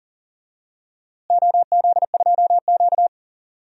Morse code, heard as a keyed tone, about 37 wpm, 700 Hz. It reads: OZ2Q
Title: DXCC-100 Call Signs - 4 Characters - Repeated 2x - 40wpm